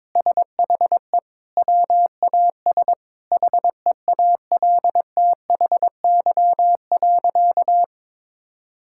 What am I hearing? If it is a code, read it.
Morse code, heard as a keyed tone, about 21 wpm, 700 Hz. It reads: SHE WAS HEALTHY.
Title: Sentences from Top 1000 Words - Review Part 5 - 22wpm